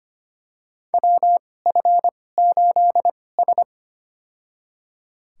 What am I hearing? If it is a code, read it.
Morse code, heard as a keyed tone, about 25 wpm, spaced as for 18 wpm, 700 Hz. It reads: WF8S
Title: USA Call Signs - 4 Characters - 2x Character Spacing - 25wpm